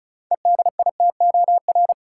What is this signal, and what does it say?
Morse code, heard as a keyed tone, about 36 wpm, 700 Hz. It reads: EDITOR
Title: Six Letter Words - 35wpm